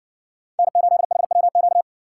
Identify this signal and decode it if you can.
Morse code, heard as a keyed tone, about 60 wpm, 700 Hz. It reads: N8HWQ